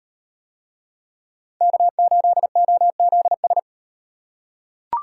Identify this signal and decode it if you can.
Morse code, heard as a keyed tone, about 38 wpm, 700 Hz. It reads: K8OZS